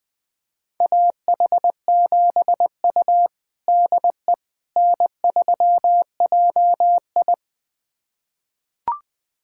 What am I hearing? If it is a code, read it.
Morse code, heard as a keyed tone, about 20 wpm, 700 Hz. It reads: AH7U DE N3JI
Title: USA Callsign DE Callsign - 4 Characters - 20wpm